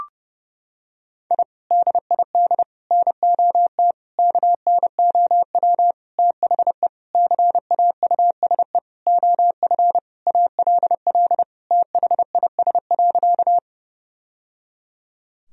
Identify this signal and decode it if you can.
Morse code, heard as a keyed tone, about 30 wpm, 700 Hz. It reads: I DID NOT KNOW THE CAUSE OF ALL THIS.
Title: Sentences from Top 1000 Words - Review Part 1 - 30wpm